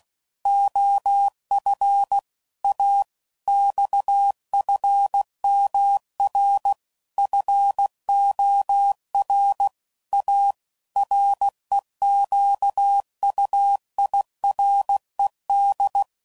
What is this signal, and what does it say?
Morse code, heard as a keyed tone, about 16 wpm, 800 Hz. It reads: OF A XFMR FOR A REQUIRED
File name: CW Code Groups @15WPM / Technical Subjects / PwrSupply